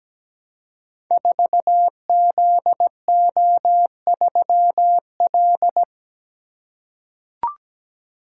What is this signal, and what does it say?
Morse code, heard as a keyed tone, about 17 wpm, 700 Hz. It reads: 4ZO3L
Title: Code Groups - 5 Letter:Numbers - 17wpm